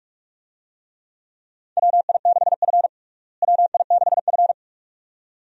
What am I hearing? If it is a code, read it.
Morse code, heard as a keyed tone, about 46 wpm, 700 Hz. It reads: WI6F WI6F
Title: USA Call Signs - 4 Characters - Repeated 2x - 45wpm